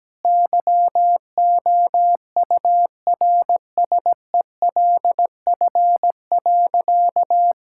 Morse code, heard as a keyed tone, about 17 wpm, 700 Hz. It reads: YOURSELF.
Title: Sentences from Top 1000 Words - Review Part 5 - 17wpm